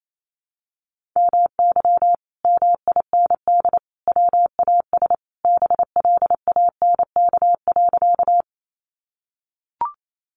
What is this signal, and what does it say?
Morse code, heard as a keyed tone, about 28 wpm, 700 Hz. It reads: MY MIND WAS BLANK.
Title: Sentences from Top 1500 Words 28wpm